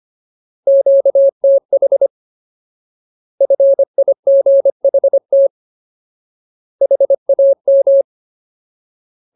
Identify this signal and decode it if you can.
Morse code, heard as a keyed tone, about 25 wpm, 550 Hz. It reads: QTH FIGHT HAM
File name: Ultimate Word List / 25WPM Ultimate